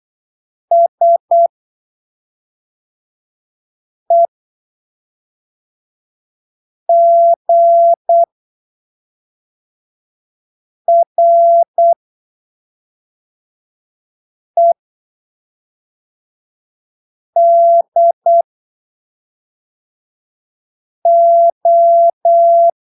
Morse code, heard as a keyed tone, about 8 wpm, spaced as for 3 wpm, 680 Hz. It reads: SEGREDO